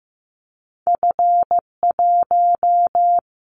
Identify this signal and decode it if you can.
Morse code, heard as a keyed tone, about 15 wpm, 700 Hz. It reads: F1
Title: Callsign Prefixes for Top 25 DXCC - 15wpm